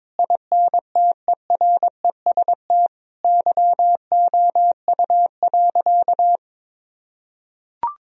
Morse code, heard as a keyed tone, about 21 wpm, 700 Hz. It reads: INTEREST YOU.
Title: Sentences from Top 1000 Words - Review Part 2 - 22wpm